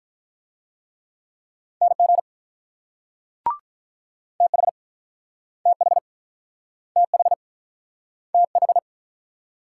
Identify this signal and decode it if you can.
Morse code, heard as a keyed tone, about 47 wpm, 700 Hz. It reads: NG TH TH TH TH